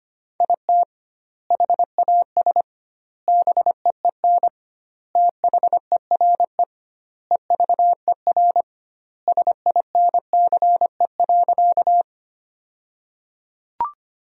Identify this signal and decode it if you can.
Morse code, heard as a keyed tone, about 25 wpm, 700 Hz. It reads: IT HAS BEEN THERE EVER SINCE.
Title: 2x Word-Spacing - Sentences from Top 400 Words - 25wpm